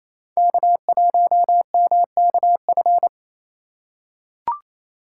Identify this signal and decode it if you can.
Morse code, heard as a keyed tone, about 28 wpm, 700 Hz. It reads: K1MKF
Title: USA Call Signs - 4x Character Spacing - 28wpm